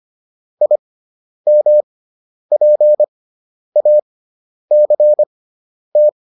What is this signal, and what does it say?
Morse code, heard as a keyed tone, about 25 wpm, spaced as for 10 wpm, 600 Hz. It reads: IMPACT
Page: intermediate-practice-files